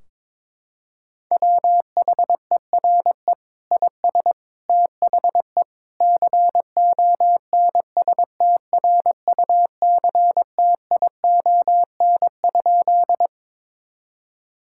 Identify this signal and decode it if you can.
Morse code, heard as a keyed tone, about 21 wpm, 700 Hz. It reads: WHERE IS THE CONSTRUCTION?
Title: Sentences from Top 1600 Words 22wpm